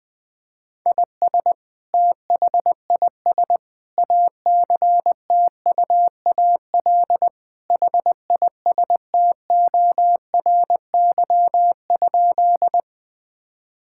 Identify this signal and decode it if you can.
Morse code, heard as a keyed tone, about 20 wpm, 700 Hz. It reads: IS THIS ACTUAL HISTORY?
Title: Sentences from Top 1000 Words - Review Part 5 - 20wpm